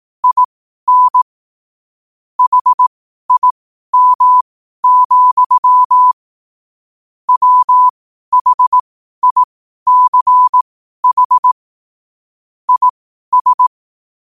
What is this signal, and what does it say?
Morse code, heard as a keyed tone, about 18 wpm, spaced as for 12 wpm, 1000 Hz. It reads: IN HIM, WHICH IS
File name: qotdfiles / 12wpm